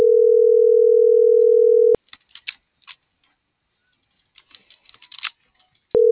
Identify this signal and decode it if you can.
Ringback tone (North America), 440 + 480 Hz.